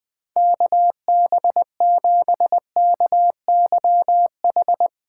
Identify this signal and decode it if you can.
Morse code, heard as a keyed tone, about 20 wpm, 700 Hz. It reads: KB7KYH
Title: USA Call Signs - 6 Characters - 4x Character Spacing - 20wpm